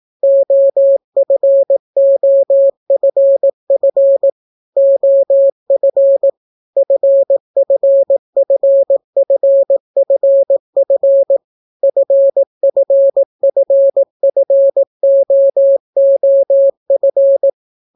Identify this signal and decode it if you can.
Morse code, heard as a keyed tone, about 18 wpm, 550 Hz. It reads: OFOFF OF FFFFFF FFFFOOF